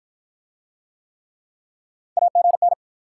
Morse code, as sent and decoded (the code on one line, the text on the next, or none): .- --. -.
AGN